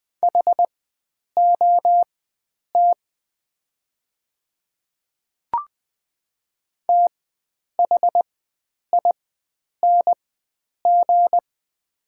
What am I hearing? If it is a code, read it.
Morse code, heard as a keyed tone, about 20 wpm, spaced as for 9 wpm, 700 Hz. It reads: HOT THING
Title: 4x Character Spacing - Top 100 Words - 20wpm